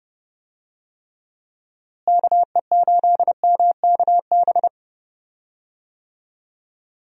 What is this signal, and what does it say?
Morse code, heard as a keyed tone, about 32 wpm, 700 Hz. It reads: KE8MKB
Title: USA Call Signs - 6 Characters - Repeated 2x - 30wpm